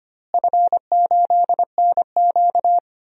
Morse code, heard as a keyed tone, about 25 wpm, 700 Hz. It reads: F8NQ
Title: DXCC-100 Call Signs - 4 Characters - Repeated 3x - 25wpm